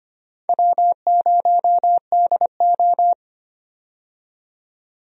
Morse code, heard as a keyed tone, about 25 wpm, 700 Hz. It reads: W0DO
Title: USA Call Signs - 4 Characters - 25wpm